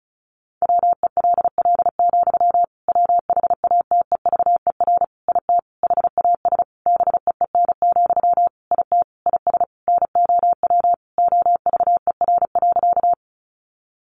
Morse code, heard as a keyed tone, about 34 wpm, 700 Hz. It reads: WELL, WHATEVER IT HAS BEEN, IT IS NOW OVER.
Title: Sentences from Top 1100 Words 35wpm